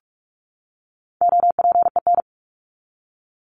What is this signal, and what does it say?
Morse code, heard as a keyed tone, about 44 wpm, 700 Hz. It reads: OPEN
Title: Top 300 Words 45wpm